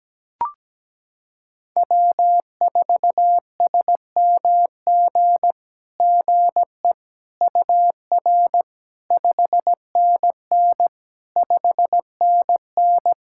Morse code, heard as a keyed tone, about 17 wpm, 700 Hz. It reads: W4SMG GE UR 5NN 5NN
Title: SOTA QSO Practice 17wpm